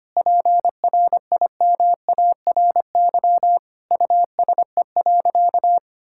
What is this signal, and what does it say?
Morse code, heard as a keyed tone, about 25 wpm, 700 Hz. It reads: PRIMARY USE.
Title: Sentences from Top 1000 Words - Review Part 5 - 25wpm